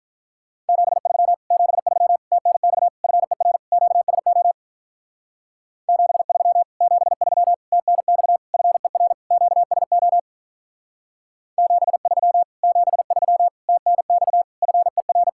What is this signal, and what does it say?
Morse code, heard as a keyed tone, about 44 wpm, 700 Hz. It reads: 73 73 TNX FER QSO 73 73 TNX FER QSO 73 73 TNX FER